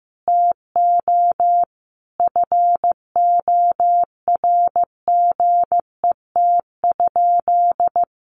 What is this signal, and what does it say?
Morse code, heard as a keyed tone, about 15 wpm, 700 Hz. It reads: TO FORGET?
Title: Sentences from Top 1200 Words 15wpm